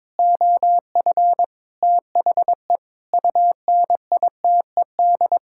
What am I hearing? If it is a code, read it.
Morse code, heard as a keyed tone, about 22 wpm, 700 Hz. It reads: OF THE UNITED
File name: practice / 22wpm / Sentences with 4-Word Phrases - 22wpm